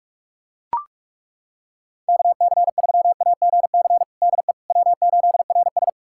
Morse code, heard as a keyed tone, about 42 wpm, 700 Hz. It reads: KC3AGC DE W8RS
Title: USA Callsign DE Callsign 45wpm